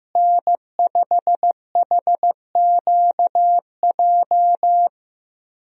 Morse code, heard as a keyed tone, about 15 wpm, 700 Hz. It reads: N5HQJ
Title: USA Call Signs - 5 Characters - Repeated 3x - 15wpm